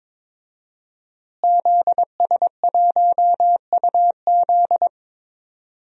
Morse code, heard as a keyed tone, about 22 wpm, 700 Hz. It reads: ZS1UZ
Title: DXCC-100 Call Signs - 5 Characters - Repeated 2x - 22wpm